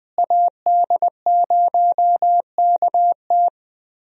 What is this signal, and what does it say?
Morse code, heard as a keyed tone, about 20 wpm, 700 Hz. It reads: AD0KT
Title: USA Call Signs - 5 Characters - Repeated 2x - 20wpm